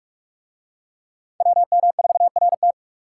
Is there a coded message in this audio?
Yes: Morse code from 1.4 to 2.7 s, heard as a keyed tone, about 41 wpm, 700 Hz. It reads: WM4RT